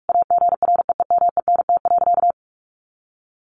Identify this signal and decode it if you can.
Morse code, heard as a keyed tone, about 44 wpm, 700 Hz. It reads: AGREEMENT.
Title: Sentences from Top 1500 Words 45wpm